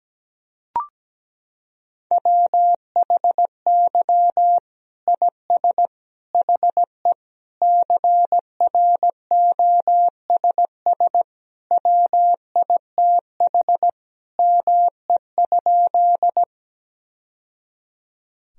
Morse code, heard as a keyed tone, about 17 wpm, 700 Hz. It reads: WHY IS HE CROSS WITH ME?
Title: Sentences from Top 1000 Words - Review Part 1 - 17wpm